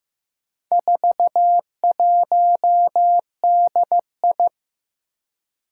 Morse code, heard as a keyed tone, about 15 wpm, 700 Hz. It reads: 41DI
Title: Four Challenging Letter-Number Combinations 15wpm